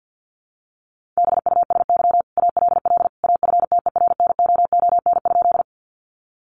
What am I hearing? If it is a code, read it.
Morse code, heard as a keyed tone, about 50 wpm, 700 Hz. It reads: BUSY ALL AFTERNOON?
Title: Sentences from Top 1500 Words 50wpm